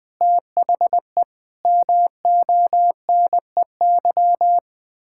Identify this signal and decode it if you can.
Morse code, heard as a keyed tone, about 20 wpm, 700 Hz. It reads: THE MONEY
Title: Pangram Challenge v2 - 20wpm